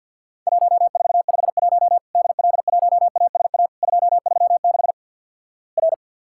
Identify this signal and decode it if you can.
Morse code, heard as a keyed tone, about 50 wpm, 700 Hz. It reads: 1451 DL1ASA 236 R
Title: Worked All Europe DX Contest - QTC Exchange - 50wpm